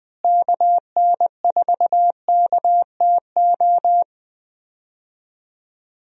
Morse code, heard as a keyed tone, about 19 wpm, 700 Hz. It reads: KN4KTO
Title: USA Call Signs - 6 Characters - Repeated 3x - 20wpm